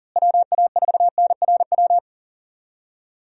Morse code, heard as a keyed tone, about 37 wpm, 700 Hz. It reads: WA4NRW
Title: USA Call Signs - 6 Characters - Repeated 3x - 40wpm